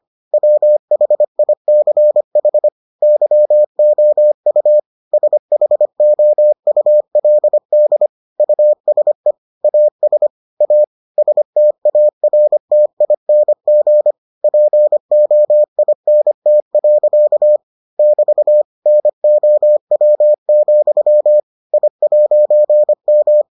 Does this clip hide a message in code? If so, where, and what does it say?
Yes: Morse code from 0.3 to 23.4 s, heard as a keyed tone, about 25 wpm, 600 Hz. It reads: WHICH YOU SHOULD USE AS A STARTING POINT. <BT> NOW, I'M